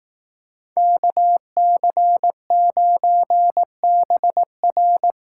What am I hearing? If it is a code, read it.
Morse code, heard as a keyed tone, about 18 wpm, 700 Hz. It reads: KC9BR
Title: Speed Racing - USA 2x2 Call Signs 15wpm